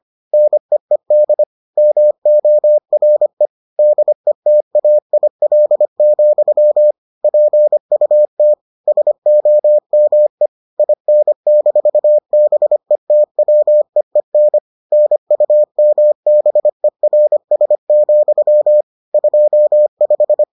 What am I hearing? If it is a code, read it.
Morse code, heard as a keyed tone, about 25 wpm, 600 Hz. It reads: NEED MORE DETAIL, PUT SOME IN-BETWEEN NUMBERS, 25